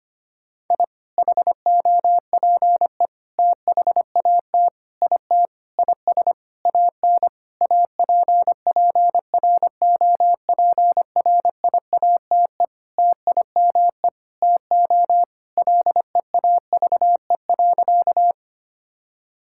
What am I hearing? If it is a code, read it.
Morse code, heard as a keyed tone, about 25 wpm, 700 Hz. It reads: I HOPE THAT IT IS AN APPROPRIATE TIME TO LEAVE.
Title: Sentences from Top 1000 Words - Review Part 5 - 25wpm